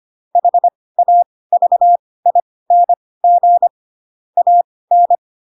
Morse code, heard as a keyed tone, about 25 wpm, spaced as for 18 wpm, 700 Hz. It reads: HAVING AN
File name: Pinocchio / Pinocchio ch2 12 25 ik1hge